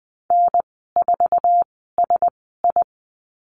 Morse code, heard as a keyed tone, about 20 wpm, spaced as for 15 wpm, 700 Hz. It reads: N4SI
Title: 088 Lesson 15g - Number 5 - Call Signs 20wpm